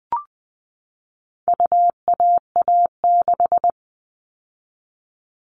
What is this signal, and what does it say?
Morse code, heard as a keyed tone, about 20 wpm, 700 Hz. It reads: UAA6
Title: All Cut Numbers - 4-Digit 20wpm